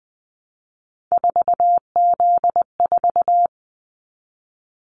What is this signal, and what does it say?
Morse code, heard as a keyed tone, about 20 wpm, 700 Hz. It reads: 4Z4
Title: Callsign Prefixes for Top 50 DXCC - 20wpm